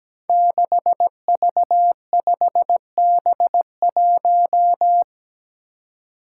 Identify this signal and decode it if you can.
Morse code, heard as a keyed tone, about 17 wpm, 700 Hz. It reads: 6V5B1
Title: Five Challenging Letter-Number Combinations 17wpm